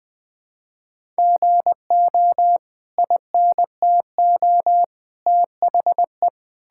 Morse code, heard as a keyed tone, about 21 wpm, 700 Hz. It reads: GO INTO THE
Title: Sentences from Top 1000 Words - Review Part 5 - 20wpm